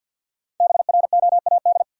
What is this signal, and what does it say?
Morse code, heard as a keyed tone, about 50 wpm, 700 Hz. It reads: BROAD
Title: Sentences from Top 1000 Words - New Word Drill - 50wpm